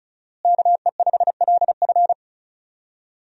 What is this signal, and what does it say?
Morse code, heard as a keyed tone, about 36 wpm, 700 Hz. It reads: KE5LF
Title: USA Call Signs - 5 Characters - Repeated 2x - 35wpm